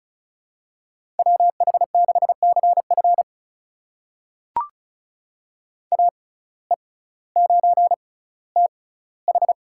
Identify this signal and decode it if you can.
Morse code, heard as a keyed tone, about 36 wpm, spaced as for 18 wpm, 700 Hz. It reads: WH6CF AE9TH
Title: USA Call Signs - 5 Characters - 6x Character Spacing - 35wpm